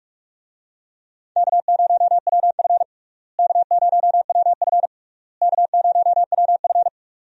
Morse code, heard as a keyed tone, about 41 wpm, 700 Hz. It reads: K0WF K0WF K0WF
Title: USA Call Signs - 4 Characters - Repeated 3x - 45wpm